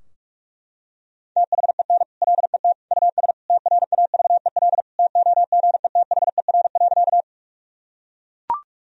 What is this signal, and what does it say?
Morse code, heard as a keyed tone, about 41 wpm, 700 Hz. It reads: THEN LET US TRAVEL TOGETHER.
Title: Sentences from Top 1000 Words - Review Part 2 - 45wpm